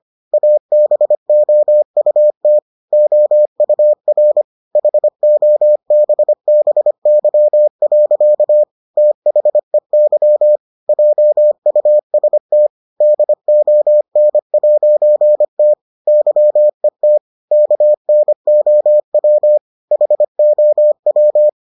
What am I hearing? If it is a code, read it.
Morse code, heard as a keyed tone, about 25 wpm, 600 Hz. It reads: ABOUT OUR HOBBY. THEY JUST DON'T YET KNOW HOW